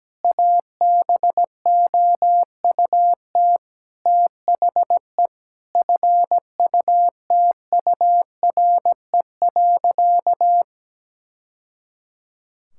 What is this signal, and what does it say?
Morse code, heard as a keyed tone, about 17 wpm, 700 Hz. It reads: ABOUT THE FUTURE.
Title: Sentences from Top 1000 Words - Review Part 5 - 17wpm